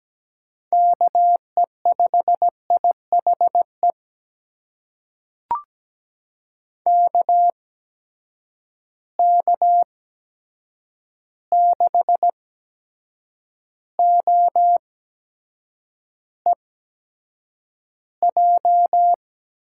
Morse code, heard as a keyed tone, about 17 wpm, spaced as for 7 wpm, 700 Hz. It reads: KE5IHE KK6OEJ
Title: USA Call Signs - 6 Characters - 8x Character Spacing - 17wpm